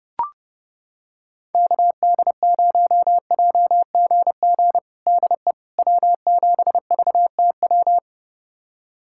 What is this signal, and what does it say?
Morse code, heard as a keyed tone, about 32 wpm, 700 Hz. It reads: KD0JGG DE W7VTW
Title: USA Callsign DE Callsign 30wpm